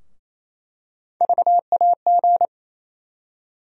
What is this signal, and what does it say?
Morse code, heard as a keyed tone, about 28 wpm, 700 Hz. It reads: VAG
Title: USA Call Sign Postfixes - 3 Letters 28wpm